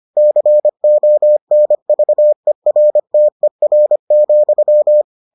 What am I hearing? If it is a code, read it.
Morse code, heard as a keyed tone, about 25 wpm, 600 Hz. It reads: CONVERTER,